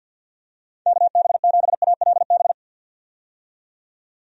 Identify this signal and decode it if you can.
Morse code, heard as a keyed tone, about 50 wpm, 700 Hz. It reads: KB7ALB
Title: USA Call Signs - 6 Characters - 50wpm